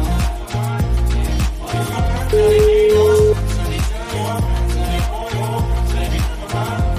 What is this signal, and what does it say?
Ringback tone (Europe, 425 Hz), with noise.